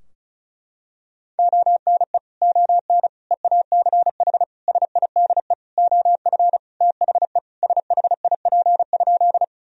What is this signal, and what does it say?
Morse code, heard as a keyed tone, about 36 wpm, 700 Hz. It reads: ONE ON EACH SIDE OF THE SHIP?
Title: Sentences from Top 1000 Words - Review Part 2 - 35wpm